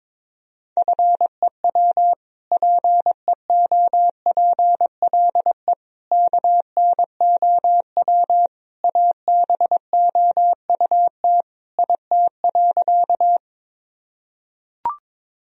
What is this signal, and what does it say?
Morse code, heard as a keyed tone, about 22 wpm, 700 Hz. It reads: FEW PEOPLE KNOW ABOUT IT.
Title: Sentences from Top 1000 Words - Review Part 1 - 22wpm